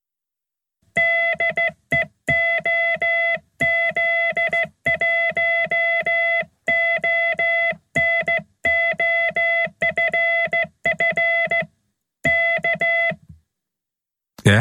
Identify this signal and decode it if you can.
Morse code, heard as a keyed tone, about 14 wpm, 2000 Hz. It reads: DE OZ1ONO FF K